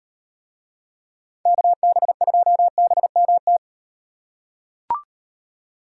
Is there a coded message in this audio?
Yes: Morse code from 1.5 to 3.6 s, heard as a keyed tone, about 38 wpm, 700 Hz. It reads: KB2BMT